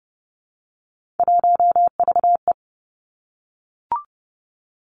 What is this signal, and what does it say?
Morse code, heard as a keyed tone, about 30 wpm, 700 Hz. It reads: 1VE
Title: Three Challenging Letter-Number Combinations 30wpm